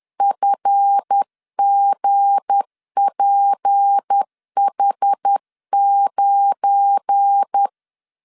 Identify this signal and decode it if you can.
Morse code, heard as a keyed tone, about 10 wpm, 800 Hz. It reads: FGPH9
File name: LEZIONI / 9 28 10wpm 2842